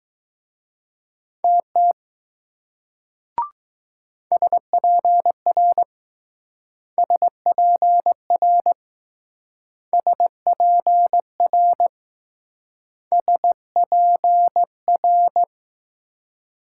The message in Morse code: TT SPR SPR SPR SPR

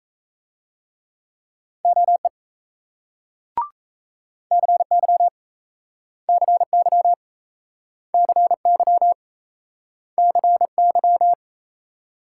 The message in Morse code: OE CY CY CY CY

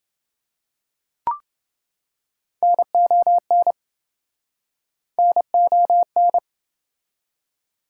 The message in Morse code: NON NON

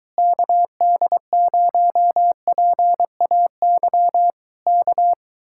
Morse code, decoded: KD0PAY K

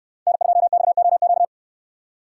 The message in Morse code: N2XOQ

